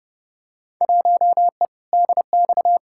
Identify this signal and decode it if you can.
Morse code, heard as a keyed tone, about 30 wpm, 700 Hz. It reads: 1E DX
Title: Field Day Exchanges Encore - 30wpm